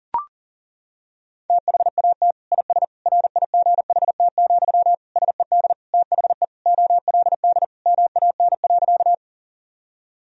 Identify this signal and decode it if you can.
Morse code, heard as a keyed tone, about 37 wpm, 700 Hz. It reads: THAT IS RIGHT, SED THE OLD MAN.
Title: Sentences from Top 200 Words and Abbreviations 40wpm